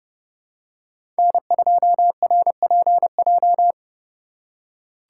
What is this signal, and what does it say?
Morse code, heard as a keyed tone, about 30 wpm, 700 Hz. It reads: N2RPJ
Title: USA Call Signs - 2x Character Spacing - 30wpm